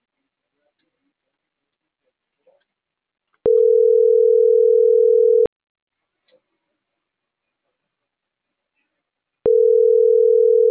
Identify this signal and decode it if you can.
Ringback tone (North America), 440 + 480 Hz.